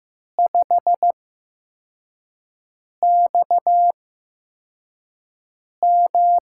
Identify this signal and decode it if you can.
Morse code, heard as a keyed tone, about 15 wpm, spaced as for 4 wpm, 700 Hz. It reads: 5XM
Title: USA Call Signs - 4 Characters - 8x Character Spacing - 15wpm